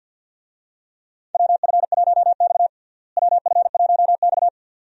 Morse code, heard as a keyed tone, about 49 wpm, 700 Hz. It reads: WF1X WF1X
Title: USA Call Signs - 4 Characters - Repeated 2x - 50wpm